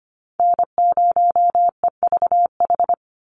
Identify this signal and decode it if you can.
Morse code, heard as a keyed tone, about 25 wpm, 700 Hz. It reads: N0EVH
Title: NA SOTA Chaser Callsigns 25wpm